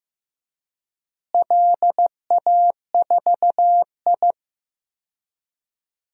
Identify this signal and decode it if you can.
Morse code, heard as a keyed tone, about 15 wpm, 700 Hz. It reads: LA4I
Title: DXCC-100 Call Signs - 4 Characters - Repeated 3x - 15wpm